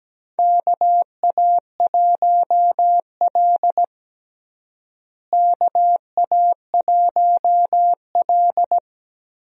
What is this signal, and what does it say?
Morse code, heard as a keyed tone, about 17 wpm, 700 Hz. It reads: KA1L KA1L
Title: USA Call Signs - 4 Characters - Repeated 2x - 17wpm